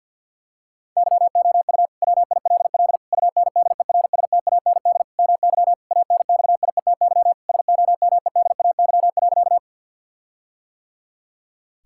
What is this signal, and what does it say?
Morse code, heard as a keyed tone, about 50 wpm, 700 Hz. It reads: YOU WILL UNDERSTAND MY ANXIETY SOMEDAY.
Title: Sentences from Top 1600 Words 50wpm